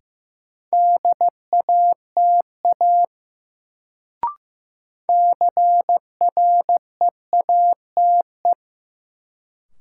Morse code, heard as a keyed tone, about 15 wpm, 700 Hz. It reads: DATA CREATE
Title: Sentences from Top 1000 Words - Review Part 3 - New Word Drill - 15wpm